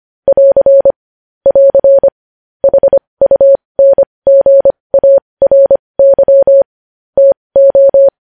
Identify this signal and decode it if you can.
Morse code, heard as a keyed tone, about 25 wpm, spaced as for 20 wpm, 550 Hz. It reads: <AR> <AR> HUNGARY TO
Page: news headlines in morse code at 20 wpm